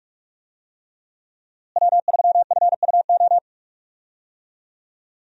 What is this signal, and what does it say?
Morse code, heard as a keyed tone, about 44 wpm, 700 Hz. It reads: W3FUO
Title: USA Call Signs - 5 Characters - Repeated 3x - 45wpm